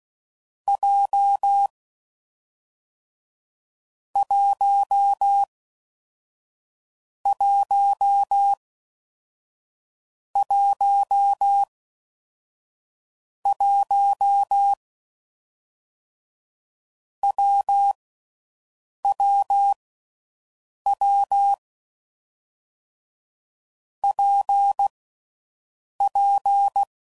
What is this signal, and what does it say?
Morse code, heard as a keyed tone, about 16 wpm, 800 Hz. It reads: J 1 1 1 1 W W W P P
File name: Combinations / Difficult Combos / WPJ WPJ1